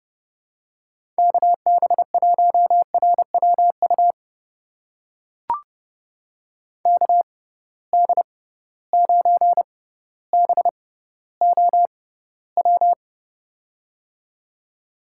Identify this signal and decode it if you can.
Morse code, heard as a keyed tone, about 30 wpm, spaced as for 15 wpm, 700 Hz. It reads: KB1RWU KD9BOW
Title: USA Call Signs - 6 Characters - 6x Character Spacing - 30wpm